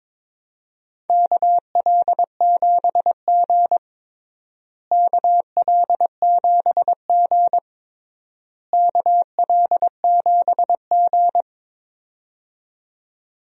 Morse code, heard as a keyed tone, about 22 wpm, 700 Hz. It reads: KL7G KL7G KL7G
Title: DXCC-100 Call Signs - 4 Characters - Repeated 3x - 22wpm